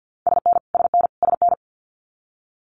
Morse code, heard as a keyed tone, about 50 wpm, 700 Hz. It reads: SN SN SN
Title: Multi-letter Phonemes Repeated 3x 50wpm